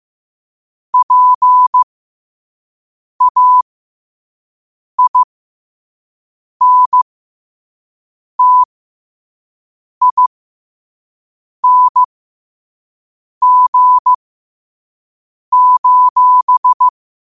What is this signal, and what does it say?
Morse code, heard as a keyed tone, about 15 wpm, spaced as for 5 wpm, 1000 Hz. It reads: PAINTING:
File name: qotdfiles / 5wpm